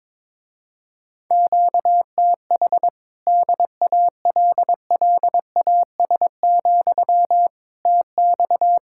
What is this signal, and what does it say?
Morse code, heard as a keyed tone, about 23 wpm, 700 Hz. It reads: QTH DALLAS, TX
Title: Speed-Racing Common QSO Phrases - 22wpm